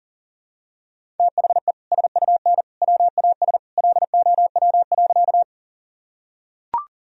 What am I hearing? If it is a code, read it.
Morse code, heard as a keyed tone, about 37 wpm, 700 Hz. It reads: THE SUN WAS LOW.
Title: Sentences from Top 1000 Words - Review Part 1 - 40wpm